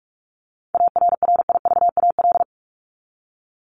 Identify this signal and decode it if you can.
Morse code, heard as a keyed tone, about 44 wpm, 700 Hz. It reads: ARRIVAL
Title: Top 1751-2000 Words 45wpm